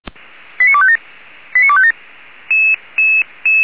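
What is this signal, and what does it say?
Five-tone selective call (ZVEI), heard as quick runs of five tones, tones of about 70 ms. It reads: 89257 89257
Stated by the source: ZVEI five-tone selective call 89469 received on SDR.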